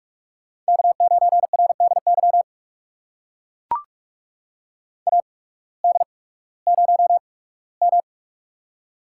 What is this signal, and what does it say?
Morse code, heard as a keyed tone, about 44 wpm, spaced as for 21 wpm, 700 Hz. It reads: K9RDY AD0M
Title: USA Call Signs - 8x Character Spacing - 45wpm